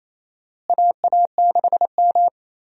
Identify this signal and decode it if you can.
Morse code, heard as a keyed tone, about 28 wpm, 700 Hz. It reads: AA6M
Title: USA Call Signs - 4 Characters - Repeated 3x - 28wpm